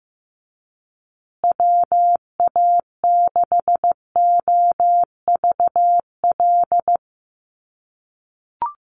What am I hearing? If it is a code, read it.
Morse code, heard as a keyed tone, about 15 wpm, 700 Hz. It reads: WA6OVL
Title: USA Call Signs Repeated 2x 15wpm